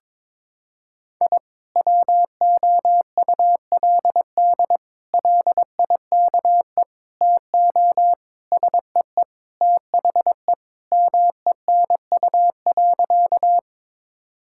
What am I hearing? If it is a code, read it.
Morse code, heard as a keyed tone, about 22 wpm, 700 Hz. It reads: I WOULD LIKE TO SEE THE MENU.
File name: practice / 22wpm / Sentences with 4-Word Phrases - 22wpm